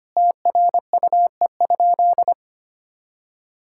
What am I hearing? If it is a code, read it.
Morse code, heard as a keyed tone, about 25 wpm, 700 Hz. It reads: TRUE?
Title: Sentences from Top 1000 Words - Review Part 6 - 25wpm